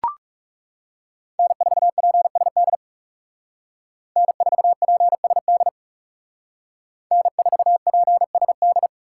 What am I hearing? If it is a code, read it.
Morse code, heard as a keyed tone, about 37 wpm, 700 Hz. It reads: N4PSD N4PSD N4PSD
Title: Speed Racing - USA 1x3 Call Signs 30wpm